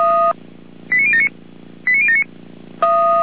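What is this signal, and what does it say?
Five-tone selective call (ZVEI), heard as quick runs of five tones, tones of about 80 ms. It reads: 89079 89079
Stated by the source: ZVEI five-tone selective call received on SDR.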